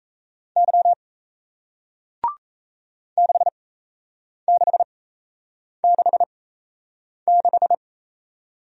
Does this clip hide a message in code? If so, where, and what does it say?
Yes: Morse code from 0.6 to 7.7 s, heard as a keyed tone, about 38 wpm, spaced as for 8 wpm, 700 Hz. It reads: Y 6666